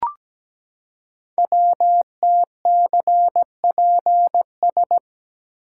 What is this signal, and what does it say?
Morse code, heard as a keyed tone, about 17 wpm, 700 Hz. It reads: WTCPS
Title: Code Groups - 5 Letters - 17wpm